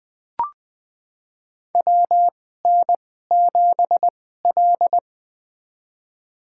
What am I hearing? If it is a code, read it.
Morse code, heard as a keyed tone, about 20 wpm, spaced as for 14 wpm, 700 Hz. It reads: WN7L
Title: USA Call Signs - 4 Characters - 2x Character Spacing - 20wpm